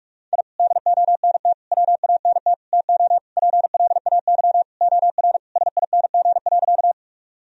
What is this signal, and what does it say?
Morse code, heard as a keyed tone, about 44 wpm, 700 Hz. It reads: I DONT WANT TO PLAY OR SING.
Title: Sentences from Top 1000 Words - Review Part 2 - 45wpm